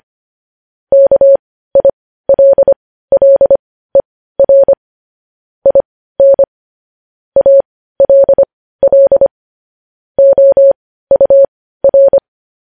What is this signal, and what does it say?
Morse code, heard as a keyed tone, about 25 wpm, spaced as for 15 wpm, 550 Hz. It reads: KILLER IN ALL OUR
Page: news-headlines-in-morse-code-at-15-wpm